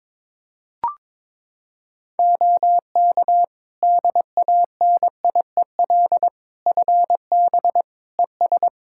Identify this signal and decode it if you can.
Morse code, heard as a keyed tone, about 22 wpm, 700 Hz. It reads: OK DANIEL FB ES
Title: K1USN SST QSO Practice 22wpm